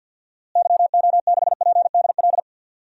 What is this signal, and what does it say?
Morse code, heard as a keyed tone, about 52 wpm, 700 Hz. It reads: YO6PDL
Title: DXCC-100 Call Signs - 6 Characters - Repeated 2x - 50wpm